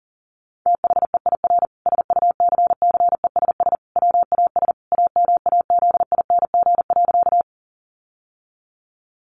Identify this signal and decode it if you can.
Morse code, heard as a keyed tone, about 40 wpm, 700 Hz. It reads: THEIR SUCCESS WAS AMAZING.
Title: Sentences from Top 1500 Words 40wpm